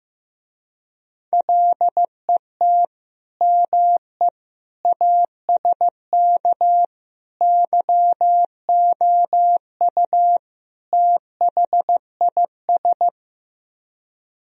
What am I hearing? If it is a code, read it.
Morse code, heard as a keyed tone, about 15 wpm, 700 Hz. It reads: LET ME ASK YOU THIS
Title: Speed-Racing - Sets of 5 Words 15wpm